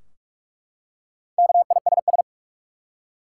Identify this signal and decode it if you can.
Morse code, heard as a keyed tone, about 46 wpm, 700 Hz. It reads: KISS
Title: Four Letter Words - 45wpm